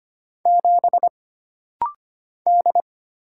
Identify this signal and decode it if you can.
Morse code, heard as a keyed tone, about 25 wpm, 700 Hz. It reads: 7 D
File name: Single Character (Letter:Number:Punctuation:Prosign) - Rapid-Fire - 25wpm